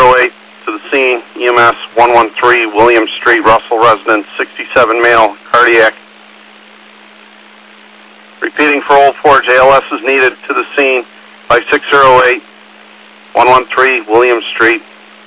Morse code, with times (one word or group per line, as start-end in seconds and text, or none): none